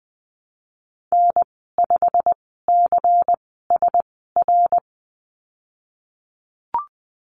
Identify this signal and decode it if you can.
Morse code, heard as a keyed tone, about 20 wpm, spaced as for 15 wpm, 700 Hz. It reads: N5CSR
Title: 128 Lesson 21g - Number 6 - Call Signs 20wpm